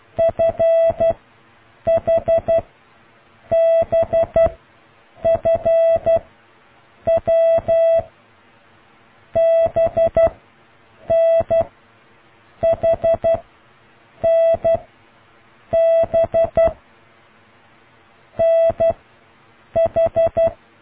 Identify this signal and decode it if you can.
Morse code, heard as a keyed tone, about 12 wpm, 660 Hz. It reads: F H B F W B N H N B N H